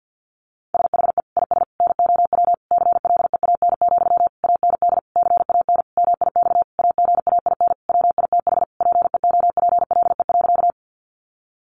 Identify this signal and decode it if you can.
Morse code, heard as a keyed tone, about 48 wpm, 700 Hz. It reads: SHE IS NOW CLEAN, AND CAN MIX AGAIN WITH PEOPLE.
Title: Sentences from Top 1100 Words 50wpm